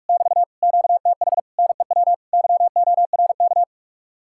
Morse code, heard as a keyed tone, about 55 wpm, 700 Hz. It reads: <BT> QTH NEW YORK
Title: Speed-Racing Common QSO Phrases Encore - 45wpm